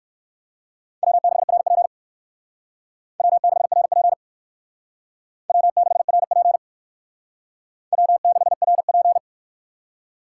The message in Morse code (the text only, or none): W6RP W6RP W6RP W6RP